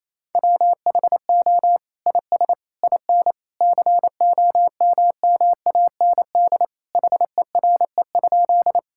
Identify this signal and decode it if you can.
Morse code, heard as a keyed tone, about 28 wpm, 700 Hz. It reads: WHO IS IN COMMAND HERE?
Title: Sentences from Top 1600 Words 28wpm